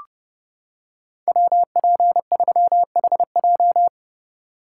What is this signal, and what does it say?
Morse code, heard as a keyed tone, about 29 wpm, 700 Hz. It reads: WP3HJ